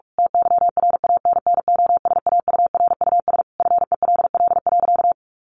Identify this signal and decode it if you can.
Morse code, heard as a keyed tone, about 44 wpm, 700 Hz. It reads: TYRANNOSAURUS FELL.
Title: A Sound Of Thunder 45wpm